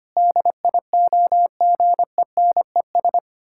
Morse code, heard as a keyed tone, about 25 wpm, 700 Hz. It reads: DIOGENES
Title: Pangram Challenge v2 - 25wpm